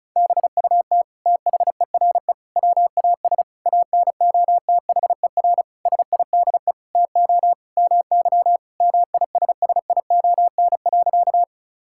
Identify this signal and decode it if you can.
Morse code, heard as a keyed tone, about 36 wpm, 700 Hz. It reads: BUT THERE WAS ANOTHER SIDE TO MY MISSION.
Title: Sentences from Top 1600 Words 35wpm